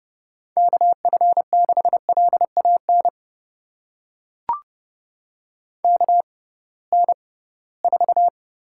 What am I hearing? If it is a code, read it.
Morse code, heard as a keyed tone, about 29 wpm, spaced as for 19 wpm, 700 Hz. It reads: KF6LAN KN4
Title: USA Call Signs - 6 Characters - 6x Character Spacing - 30wpm